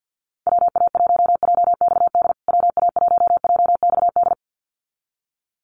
Morse code, heard as a keyed tone, about 50 wpm, 700 Hz. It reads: WA1JXD WA1JXD
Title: USA Call Signs Repeated 2x 50wpm